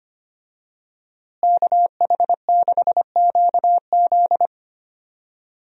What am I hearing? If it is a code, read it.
Morse code, heard as a keyed tone, about 25 wpm, 700 Hz. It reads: KH6QZ
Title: USA Call Signs - 5 Characters - Repeated 2x - 25wpm